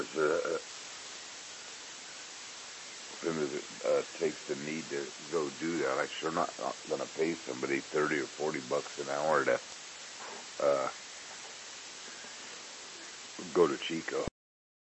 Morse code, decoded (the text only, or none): none